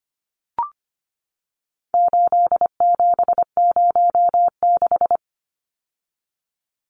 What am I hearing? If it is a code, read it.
Morse code, heard as a keyed tone, about 25 wpm, 700 Hz. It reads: 8706
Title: Numbers - 4-Digit 25wpm